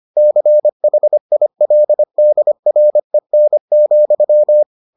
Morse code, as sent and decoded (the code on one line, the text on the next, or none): -.-. .... .. .-.. -.. .-. -. --..--
CHILDREN,